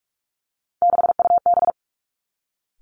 Morse code, heard as a keyed tone, about 46 wpm, 700 Hz. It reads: 6UB
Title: Three Challenging Letter-Number Combinations 45wpm